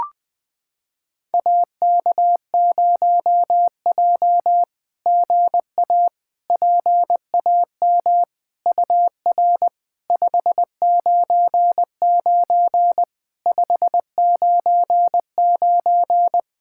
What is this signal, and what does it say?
Morse code, heard as a keyed tone, about 21 wpm, 690 Hz. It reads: AK0J GA PAM UR 599 599
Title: SOTA QSO Practice 20wpm